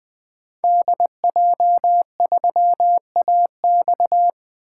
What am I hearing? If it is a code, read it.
Morse code, heard as a keyed tone, about 20 wpm, 700 Hz. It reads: DJ3AX
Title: 2023 SOTA Activator Call Signs - 20wpm